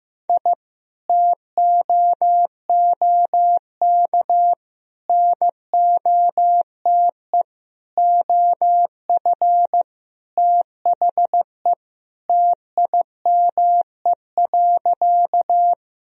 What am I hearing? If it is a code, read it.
Morse code, heard as a keyed tone, about 15 wpm, 700 Hz. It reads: I TOOK NOTE OF THE TIME.
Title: Sentences from Top 1000 Words - Review Part 2 - 15wpm